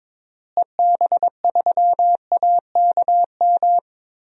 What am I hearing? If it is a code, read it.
Morse code, heard as a keyed tone, about 22 wpm, 700 Hz. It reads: EB3AKM
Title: DXCC-100 Call Signs - 6 Characters - Repeated 3x - 22wpm